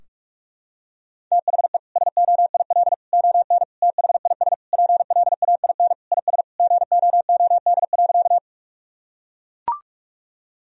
Morse code, heard as a keyed tone, about 41 wpm, 700 Hz. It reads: THE SOIL ON THIS PLAIN IS GOOD.
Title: Sentences from Top 1000 Words - Review Part 4 - 45wpm